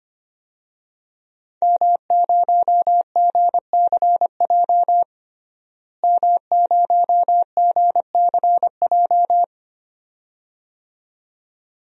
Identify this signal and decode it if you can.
Morse code, heard as a keyed tone, about 25 wpm, 700 Hz. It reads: M0GCJ M0GCJ
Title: DXCC-100 Call Signs - 5 Characters - Repeated 2x - 25wpm